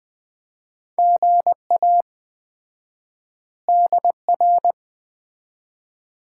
Morse code, heard as a keyed tone, about 20 wpm, 700 Hz. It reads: GA DR
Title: Common QSO Phrases - 4x Word-Spacing - 20wpm